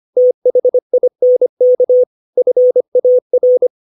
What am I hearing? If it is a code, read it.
Morse code, heard as a keyed tone, about 25 wpm, 500 Hz. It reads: THINK FAR
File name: 632 words / 632 Words 25WPM